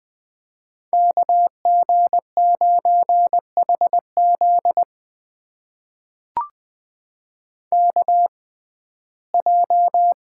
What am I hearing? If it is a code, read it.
Morse code, heard as a keyed tone, about 20 wpm, spaced as for 15 wpm, 700 Hz. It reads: KG9HZ KJ
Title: USA Call Signs - 6x Character Spacing - 20wpm